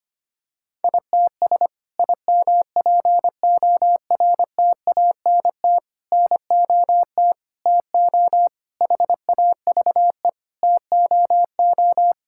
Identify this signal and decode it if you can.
Morse code, heard as a keyed tone, about 25 wpm, 700 Hz. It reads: ITS IMPORTANT NOT TO HAVE TOO
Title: Sentences from Top 1000 Words - Review Part 5 - 25wpm